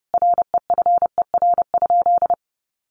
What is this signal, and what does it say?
Morse code, heard as a keyed tone, about 30 wpm, 700 Hz. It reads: REFER?
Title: Sentences from Top 1000 Words 30wpm